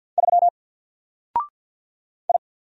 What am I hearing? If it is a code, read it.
Morse code, heard as a keyed tone, about 48 wpm, 700 Hz. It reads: 3 I
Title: Single Character (Letter:Number:Punctuation:Prosign) 50wpm